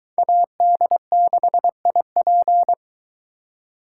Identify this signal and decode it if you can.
Morse code, heard as a keyed tone, about 22 wpm, 700 Hz. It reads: AD6IP